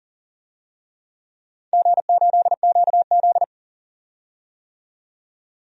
Morse code, heard as a keyed tone, about 40 wpm, 700 Hz. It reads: G8QZ
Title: DXCC-100 Call Signs - 4 Characters - 40wpm